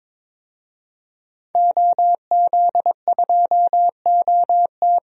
Morse code, heard as a keyed tone, about 22 wpm, 700 Hz. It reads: OZ2OT
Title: DXCC-100 Call Signs - 5 Characters - 22wpm